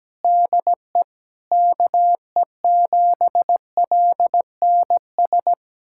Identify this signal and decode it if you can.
Morse code, heard as a keyed tone, about 17 wpm, 700 Hz. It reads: DE KE7LNS
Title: USA Callsign DE Callsign 17wpm